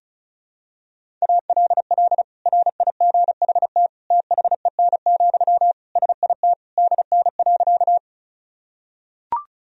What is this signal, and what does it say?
Morse code, heard as a keyed tone, about 36 wpm, 700 Hz. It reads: ALL RIGHT THEN, SIT DN.
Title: Sentences from Top 500 Words and Abbreviations 35wpm